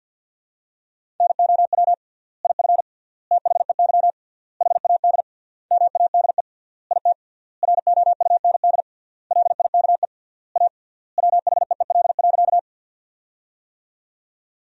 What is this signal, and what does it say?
Morse code, heard as a keyed tone, about 50 wpm, 700 Hz. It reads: NOW IF THEY HAD MADE IT ROUND LIKE A WHEEL.
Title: 3x Word-Spacing - Sentences from Top 500 Words - 50wpm